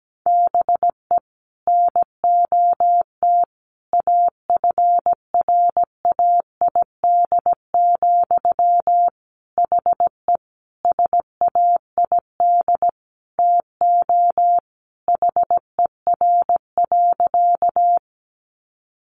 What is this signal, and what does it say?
Morse code, heard as a keyed tone, about 17 wpm, 700 Hz. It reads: BE NOT AFRAID, HE SAID TO HER.
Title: Sentences from Top 1500 Words 17wpm